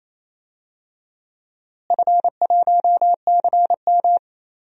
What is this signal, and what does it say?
Morse code, heard as a keyed tone, about 28 wpm, 700 Hz. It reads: F1CM
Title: DXCC-100 Call Signs - 4 Characters - Repeated 2x - 28wpm